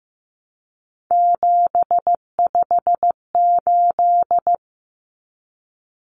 Morse code, heard as a keyed tone, about 15 wpm, 700 Hz. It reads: 758
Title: Common Cut Numbers - 3-Digit 15wpm